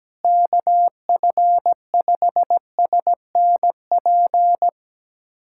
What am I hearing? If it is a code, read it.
Morse code, heard as a keyed tone, about 17 wpm, 700 Hz. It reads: KF5SNP
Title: USA Callsign DE Callsign - 6 Characters - 17wpm